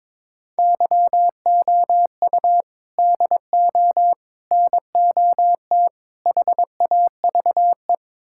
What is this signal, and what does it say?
Morse code, heard as a keyed tone, about 22 wpm, 700 Hz. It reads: YOU DO NOT HAVE
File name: practice / 22wpm / Build A Phrase - 5-Word Phrases - 22wpm